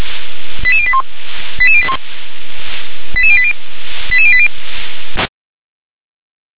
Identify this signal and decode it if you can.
Five-tone selective call (ZVEI), heard as quick runs of five tones, tones of about 70 ms. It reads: 88081 88081 88080 88080
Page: /357/SDR/ZVEI/